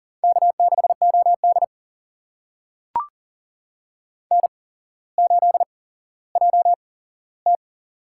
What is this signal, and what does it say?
Morse code, heard as a keyed tone, about 39 wpm, spaced as for 17 wpm, 690 Hz. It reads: K6OD N8JT
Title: USA Call Signs - 4 Characters - 8x Character Spacing - 40wpm